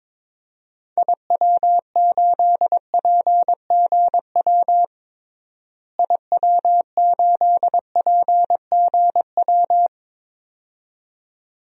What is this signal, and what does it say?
Morse code, heard as a keyed tone, about 22 wpm, 700 Hz. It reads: IW8PGW IW8PGW
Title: DXCC-100 Call Signs - 6 Characters - Repeated 2x - 22wpm